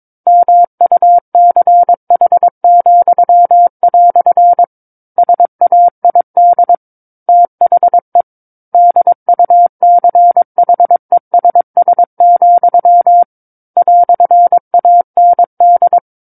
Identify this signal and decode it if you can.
Morse code, heard as a keyed tone, about 22 wpm, 700 Hz. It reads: MUCH," SAID THE DUCHESS, "AND
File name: 06-Chapter-06 22 WPM